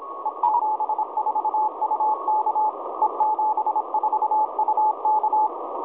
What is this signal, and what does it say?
Morse code, heard as a keyed tone, about 26 wpm, 840 Hz, with noise. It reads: E74UK E74UK
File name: 4U2STAYHOME 7016.2kHz